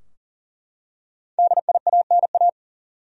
Morse code, heard as a keyed tone, about 40 wpm, 700 Hz. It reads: DIANA
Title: USA Common Names - 40wpm